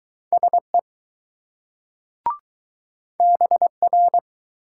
Morse code, heard as a keyed tone, about 20 wpm, 700 Hz. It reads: SE BR